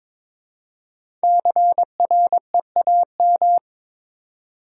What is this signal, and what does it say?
Morse code, heard as a keyed tone, about 22 wpm, 690 Hz. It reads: CREAM
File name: 5 Letter Words - First 3 Letters Given - 22wpm